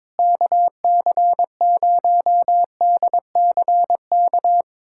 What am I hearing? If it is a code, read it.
Morse code, heard as a keyed tone, about 22 wpm, 700 Hz. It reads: KC0DCK
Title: USA Callsign DE Callsign - 6 Characters - 22wpm